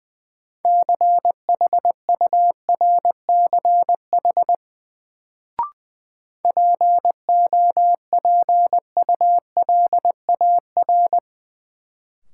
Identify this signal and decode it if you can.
Morse code, heard as a keyed tone, about 20 wpm, 700 Hz. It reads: CHURCH POPULAR
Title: Sets of 5 Words - New Word Drill - 20wpm